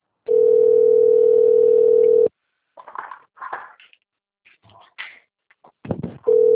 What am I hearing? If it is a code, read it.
Ringback tone (North America), 440 + 480 Hz.
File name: rg-1600-unknown- - - .13399